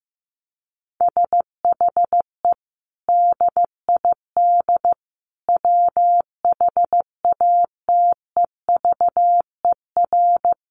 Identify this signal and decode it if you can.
Morse code, heard as a keyed tone, about 15 wpm, 700 Hz. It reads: SHE DID WHATEVER
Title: Sentences from Top 1100 Words 15wpm